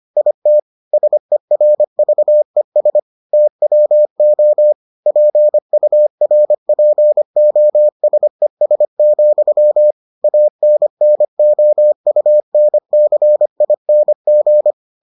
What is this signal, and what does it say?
Morse code, heard as a keyed tone, about 25 wpm, 600 Hz. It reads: IT SERVES TWO PURPOSES, ANNOUNCING